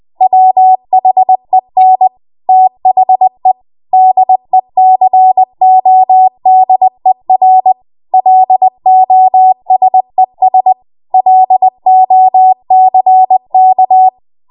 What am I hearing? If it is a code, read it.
Morse code, heard as a keyed tone, about 20 wpm, 750 Hz. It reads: WHEN THE DECODER LOSES LOCK
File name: Morse / 20WPM